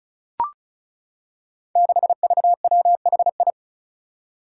Morse code, heard as a keyed tone, about 36 wpm, 700 Hz. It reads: 6VWHI
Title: Five Challenging Letter-Number Combinations 35wpm